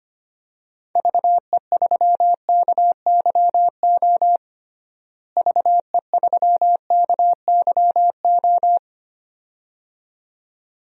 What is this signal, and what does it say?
Morse code, heard as a keyed tone, about 25 wpm, 700 Hz. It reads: VE3KYO VE3KYO
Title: DXCC-100 Call Signs - 6 Characters - Repeated 2x - 25wpm